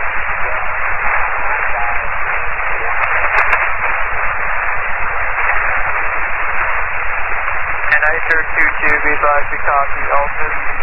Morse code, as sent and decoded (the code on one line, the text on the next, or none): none